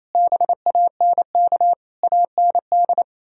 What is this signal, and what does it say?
Morse code, heard as a keyed tone, about 38 wpm, 700 Hz. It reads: BANK AND